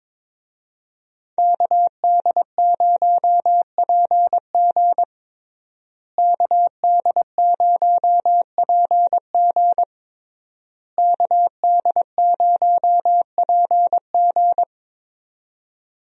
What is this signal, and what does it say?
Morse code, heard as a keyed tone, about 22 wpm, 700 Hz. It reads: KD0PG KD0PG KD0PG
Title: USA Call Signs - 5 Characters - Repeated 3x - 22wpm